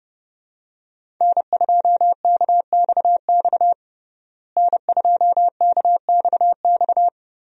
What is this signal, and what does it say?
Morse code, heard as a keyed tone, about 30 wpm, 700 Hz. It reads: N2KXX N2KXX
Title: USA Call Signs - 5 Characters - Repeated 3x - 30wpm